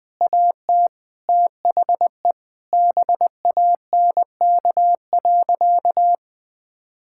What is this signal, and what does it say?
Morse code, heard as a keyed tone, about 20 wpm, 700 Hz. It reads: AT THE BANK.
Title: Sentences from Top 1000 Words - Review Part 6 - 20wpm